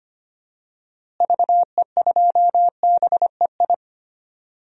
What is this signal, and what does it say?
Morse code, heard as a keyed tone, about 25 wpm, 700 Hz. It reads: VE2BEI
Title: DXCC-100 Call Signs - 6 Characters - Repeated 3x - 25wpm